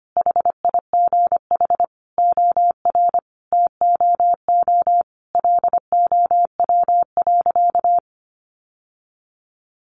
Morse code, heard as a keyed tone, about 25 wpm, 700 Hz. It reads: HIGH OR TOO LOW.
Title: Sentences from Top 1200 Words 25wpm